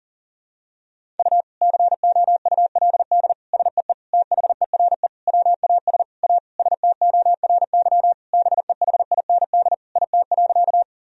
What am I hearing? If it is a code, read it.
Morse code, heard as a keyed tone, about 37 wpm, 700 Hz. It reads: U COULD SEE THERE WAS A STORY BEHIND IT.
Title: Sentences from Top 500 Words and Abbreviations 40wpm